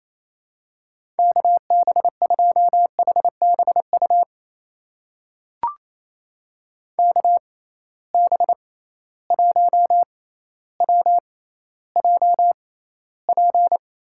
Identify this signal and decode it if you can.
Morse code, heard as a keyed tone, about 28 wpm, spaced as for 14 wpm, 700 Hz. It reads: KB2HBU KB1WJP
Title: USA Call Signs - 6 Characters - 6x Character Spacing - 28wpm